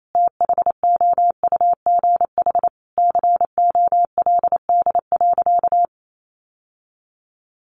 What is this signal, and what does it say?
Morse code, heard as a keyed tone, about 28 wpm, 700 Hz. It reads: THOUGH COLD.
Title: Sentences from Top 1000 Words 28wpm